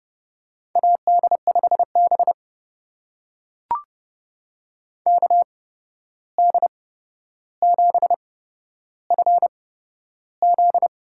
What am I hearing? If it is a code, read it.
Morse code, heard as a keyed tone, about 30 wpm, spaced as for 12 wpm, 700 Hz. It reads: AD5B KD7FZ
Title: USA Call Signs - 8x Character Spacing - 30wpm